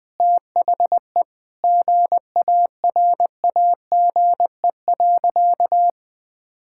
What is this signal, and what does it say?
Morse code, heard as a keyed tone, about 20 wpm, 700 Hz. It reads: THE GARAGE.
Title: Sentences from Top 1600 Words 20wpm